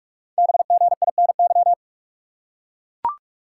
Morse code, heard as a keyed tone, about 44 wpm, 700 Hz. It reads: BGINY
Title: Code Groups - 5 Letters - 45wpm